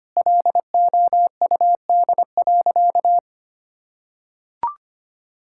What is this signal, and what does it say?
Morse code, heard as a keyed tone, about 25 wpm, 700 Hz. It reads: LOUD.
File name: practice / 25wpm / Sentences with 2-Word Phrases - 25wpm